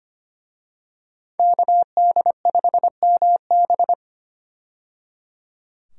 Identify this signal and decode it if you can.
Morse code, heard as a keyed tone, about 25 wpm, 700 Hz. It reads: KD5MB
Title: USA Call Signs - 5 Characters - 25wpm